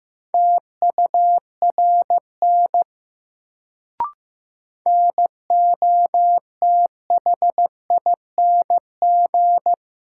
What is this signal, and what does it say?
Morse code, heard as a keyed tone, about 15 wpm, 700 Hz. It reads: TURN NOTHING
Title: Sets of 4 Words - New Word Drill - 15wpm